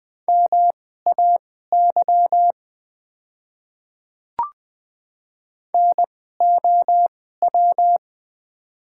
Morse code, heard as a keyed tone, about 21 wpm, spaced as for 15 wpm, 700 Hz. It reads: MAY NOW
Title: 2x Character Spacing - Top 200 Words - 20wpm